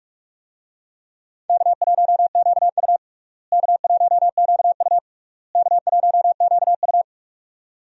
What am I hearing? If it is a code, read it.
Morse code, heard as a keyed tone, about 46 wpm, 700 Hz. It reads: K1QU K1QU K1QU
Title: USA Call Signs - 4 Characters - Repeated 3x - 45wpm